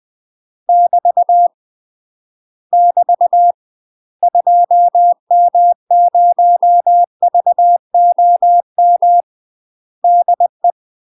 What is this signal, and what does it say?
Morse code, heard as a keyed tone, about 20 wpm, 700 Hz. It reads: <BT> <BT> 2M0VOM DE
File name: UK MASTER QSOs TRAINING / 20WPM / UK MASTER QSOs 0052